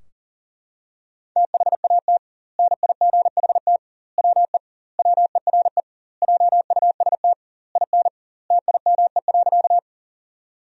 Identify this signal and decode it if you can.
Morse code, heard as a keyed tone, about 40 wpm, 700 Hz. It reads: THAT NIGHT WE WERE JUST IN TIME.
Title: 2x Word-Spacing - Sentences from Top 300 Words - 40wpm